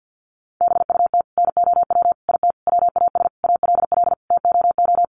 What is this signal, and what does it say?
Morse code, heard as a keyed tone, about 50 wpm, 700 Hz. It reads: BUT NOW IT WAS ALL TOO
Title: Sentences from Top 700 Words 50wpm